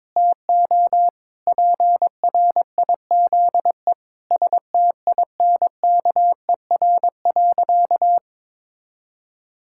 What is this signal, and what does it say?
Morse code, heard as a keyed tone, about 22 wpm, 700 Hz. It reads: TO PRIZE STINKER.
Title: Pangram Challenge v2 - 22wpm